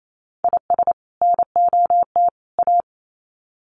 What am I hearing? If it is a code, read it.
Morse code, heard as a keyed tone, about 28 wpm, 700 Hz. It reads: IS NOT A
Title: Sets of 3 Words 28wpm